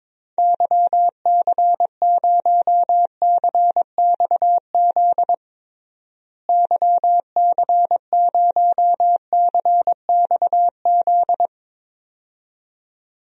Morse code, heard as a keyed tone, about 22 wpm, 700 Hz. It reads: YC0CXZ YC0CXZ
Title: DXCC-100 Call Signs - 6 Characters - Repeated 2x - 22wpm